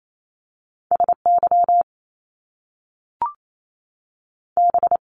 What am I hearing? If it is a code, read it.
Morse code, heard as a keyed tone, about 28 wpm, spaced as for 3 wpm, 700 Hz. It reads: S YB
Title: Top Two Letter Combinations in Words 28wpm